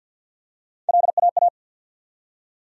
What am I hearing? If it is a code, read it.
Morse code, heard as a keyed tone, about 52 wpm, 700 Hz. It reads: LAA
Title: USA Call Sign Postfixes - 3 Letters 50wpm